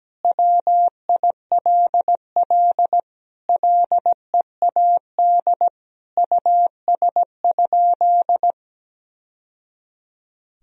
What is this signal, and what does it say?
Morse code, heard as a keyed tone, about 17 wpm, 700 Hz. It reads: WILL LEAD US?
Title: Sentences from Top 1000 Words - Review Part 2 - 17wpm